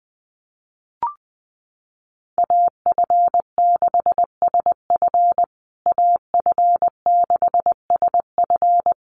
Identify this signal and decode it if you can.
Morse code, heard as a keyed tone, about 20 wpm, 700 Hz. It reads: AF6SF AF6SF
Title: USA Call Signs Repeated 3x 20wpm